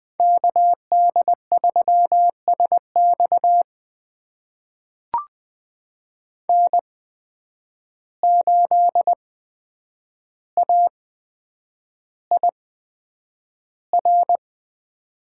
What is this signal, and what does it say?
Morse code, heard as a keyed tone, about 20 wpm, spaced as for 9 wpm, 700 Hz. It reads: KD3SX N8AIR
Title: USA Call Signs - 5 Characters - 8x Character Spacing - 20wpm